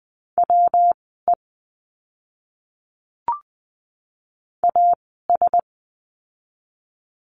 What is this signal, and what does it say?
Morse code, heard as a keyed tone, about 20 wpm, spaced as for 15 wpm, 700 Hz. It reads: WE AS
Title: 192 Lesson 31f - Letter v - Words 20wpm